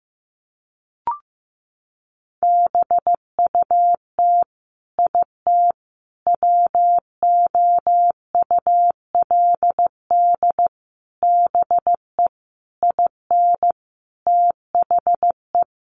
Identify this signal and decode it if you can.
Morse code, heard as a keyed tone, about 15 wpm, 700 Hz. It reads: BUT IT WOULD BE IN THE
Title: Sentences from Top 300 Words 15wpm